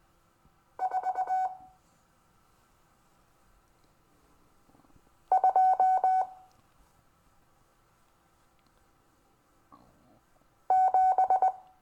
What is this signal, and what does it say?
Morse code, heard as a keyed tone, about 19 wpm, 750 Hz. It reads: 4 2 7
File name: cw / Lesson26 SUMMARY